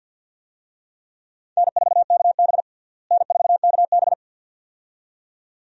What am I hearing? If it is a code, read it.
Morse code, heard as a keyed tone, about 47 wpm, 700 Hz. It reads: N4KB N4KB
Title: USA Call Signs - 4 Characters - Repeated 2x - 50wpm